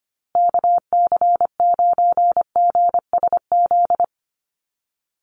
Morse code, heard as a keyed tone, about 25 wpm, 700 Hz. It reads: KC9GSZ
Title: USA Call Signs Repeated 3x 25wpm